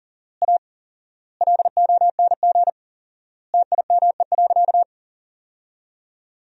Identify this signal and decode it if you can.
Morse code, heard as a keyed tone, about 37 wpm, 700 Hz. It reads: A LONG TIME.
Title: 4x Word-Spacing - Sentences from Top 400 Words - 40wpm